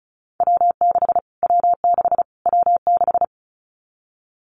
Morse code, heard as a keyed tone, about 34 wpm, 700 Hz. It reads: W6 W6 W6
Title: Callsign Prefixes for Top 25 DXCC - Repeated 3x - 35wpm